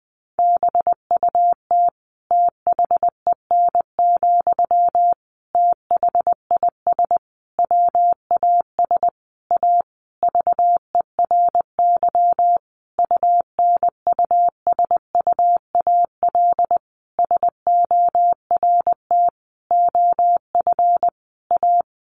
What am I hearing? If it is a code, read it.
Morse code, heard as a keyed tone, about 19 wpm, 700 Hz. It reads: BUT THEN, THIS WAS A VERY UNUSUAL SORT OF A
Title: Sentences from Top 1400 Words 20wpm